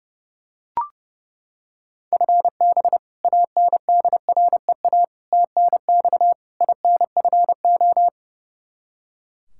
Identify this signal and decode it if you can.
Morse code, heard as a keyed tone, about 32 wpm, 700 Hz. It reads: FB ANDREA TNX INFO
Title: Practice QSO - Exchange 3 - More Info - 30wpm